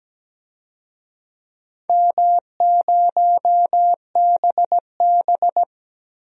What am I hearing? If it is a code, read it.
Morse code, heard as a keyed tone, about 17 wpm, 700 Hz. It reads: M0BB
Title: DXCC-100 Call Signs - 4 Characters - Repeated 3x - 17wpm